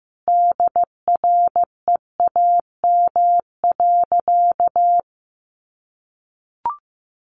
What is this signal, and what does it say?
Morse code, heard as a keyed tone, about 15 wpm, 700 Hz. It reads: DREAM.
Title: Sentences from Top 1500 Words 15wpm